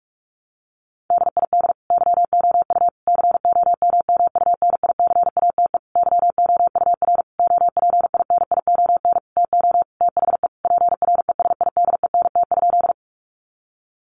Morse code, heard as a keyed tone, about 44 wpm, 700 Hz. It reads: DID YOU COMMUNICATE YOUR OPINION TO THE PRESIDENT?
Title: Sentences from Top 1300 Words 45wpm